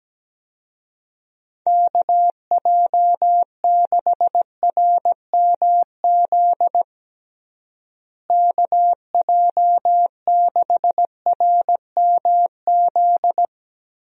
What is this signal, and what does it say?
Morse code, heard as a keyed tone, about 17 wpm, 700 Hz. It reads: KJ6RMZ KJ6RMZ
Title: USA Call Signs - 6 Characters - Repeated 3x - 17wpm